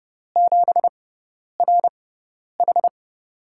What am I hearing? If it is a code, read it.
Morse code, heard as a keyed tone, about 30 wpm, spaced as for 10 wpm, 700 Hz. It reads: 7RH
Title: USA Call Signs - 6x Character Spacing - 30wpm